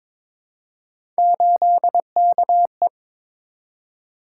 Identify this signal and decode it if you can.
Morse code, heard as a keyed tone, about 22 wpm, 700 Hz. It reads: 8KE
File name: practice / 22wpm / Three Letter-Number Combinations - 22wpm